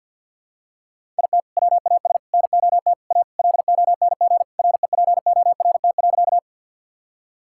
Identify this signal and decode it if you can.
Morse code, heard as a keyed tone, about 52 wpm, 700 Hz. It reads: IT WAS NOT A LONG REPORT.
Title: Sentences from Top 1000 Words - Review Part 6 - 50wpm